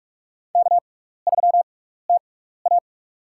Morse code, heard as a keyed tone, about 46 wpm, spaced as for 15 wpm, 700 Hz. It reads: K3TA
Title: USA Call Signs - 4 Characters - 6x Character Spacing - 45wpm